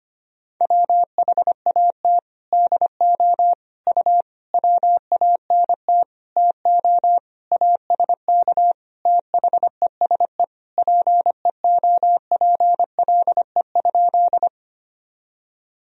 Morse code, heard as a keyed tone, about 25 wpm, 700 Hz. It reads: WHAT DO U WANT TO ASK THESE PEOPLE?
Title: Sentences from Top 200 Words and Abbreviations 25wpm